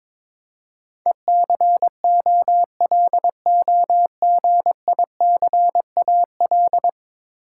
Morse code, heard as a keyed tone, about 22 wpm, 700 Hz. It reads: ECOLOGICAL
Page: morse-code-ninja-practice-22wpm